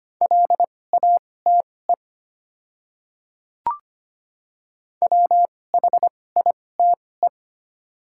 Morse code, heard as a keyed tone, about 25 wpm, spaced as for 18 wpm, 690 Hz. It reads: LATE WHITE
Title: 2x Character Spacing - Top 500 Words - 25wpm